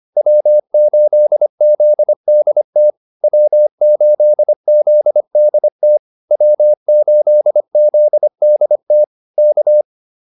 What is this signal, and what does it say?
Morse code, heard as a keyed tone, about 25 wpm, 600 Hz. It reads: W8ZDT W8ZDT W8ZDT K